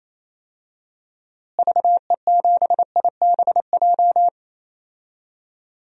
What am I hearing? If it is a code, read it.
Morse code, heard as a keyed tone, about 28 wpm, 700 Hz. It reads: VE7IBJ
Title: DXCC-100 Call Signs - 6 Characters - Repeated 3x - 28wpm